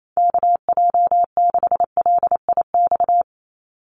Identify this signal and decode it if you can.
Morse code, heard as a keyed tone, about 28 wpm, 700 Hz. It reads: KJ6LIX
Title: USA Call Signs 28wpm